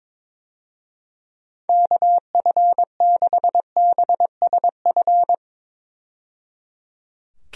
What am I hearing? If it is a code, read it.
Morse code, heard as a keyed tone, about 21 wpm, 700 Hz. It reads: KF6BSF
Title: USA Call Signs - 6 Characters - 22wpm